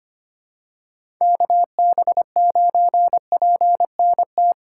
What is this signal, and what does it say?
Morse code, heard as a keyed tone, about 25 wpm, 700 Hz. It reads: KB9PNT
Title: USA Call Signs - 6 Characters - 2x Character Spacing - 25wpm